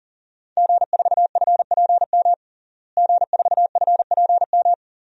Morse code, heard as a keyed tone, about 40 wpm, 700 Hz. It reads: G4FPM G4FPM
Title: DXCC-100 Call Signs - 5 Characters - Repeated 2x - 40wpm